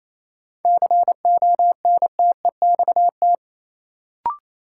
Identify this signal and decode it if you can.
Morse code, heard as a keyed tone, about 28 wpm, 700 Hz. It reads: CONTEXT
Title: Sentences from Top 1000 Words - Review Part 7 - New Word Drill - 28wpm